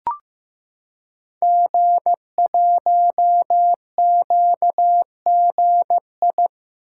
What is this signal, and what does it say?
Morse code, heard as a keyed tone, about 15 wpm, 700 Hz. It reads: G1QGI
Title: Code Groups - 5 Letter:Numbers - 15wpm